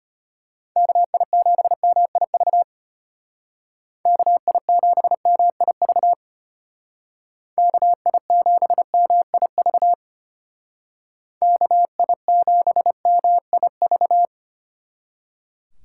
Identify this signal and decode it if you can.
Morse code, heard as a keyed tone, about 30 wpm, 700 Hz. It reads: KI7MIV KI7MIV KI7MIV KI7MIV
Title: Speed Racing - USA 2x3 Call Signs 25wpm